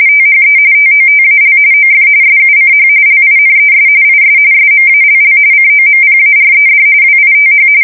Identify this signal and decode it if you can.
RTTY, about 45.45 baud. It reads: THE TOTAL OF SUCH LINES IS SMALL COMPARED WITH
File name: fsk170 / printing-telegraph